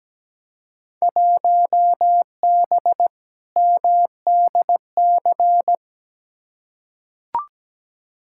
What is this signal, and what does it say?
Morse code, heard as a keyed tone, about 17 wpm, 700 Hz. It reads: 1B MDC
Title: Field Day Exchanges - 17wpm